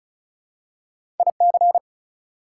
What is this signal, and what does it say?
Morse code, heard as a keyed tone, about 36 wpm, 700 Hz. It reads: IC
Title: Prefixes and Suffixes 35wpm